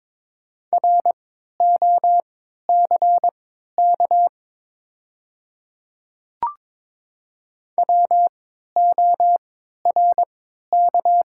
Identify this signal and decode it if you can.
Morse code, heard as a keyed tone, about 22 wpm, spaced as for 12 wpm, 690 Hz. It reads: ROCK WORK
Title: 3x Character Spacing - Top 500 Words - 22wpm